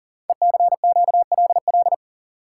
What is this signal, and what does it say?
Morse code, heard as a keyed tone, about 38 wpm, 700 Hz. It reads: ECQLL
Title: Code Groups - 5 Letters - 40wpm